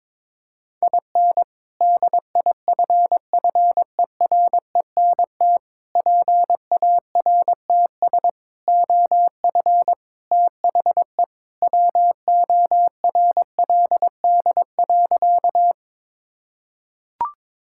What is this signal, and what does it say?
Morse code, heard as a keyed tone, about 21 wpm, 700 Hz. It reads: IN DIFFERENT PARTS OF THE WORLD.
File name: practice / 22wpm / Sentences with 3-Word Phrases - 22wpm